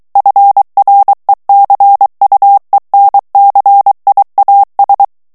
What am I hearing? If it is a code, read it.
Morse code, heard as a keyed tone, about 23 wpm, 800 Hz. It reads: FRECUENCIAS
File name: morse mp3 / ESPAGNOL TEXTE EN CLAIR 2 VITESSE 22 MOTS MINUTES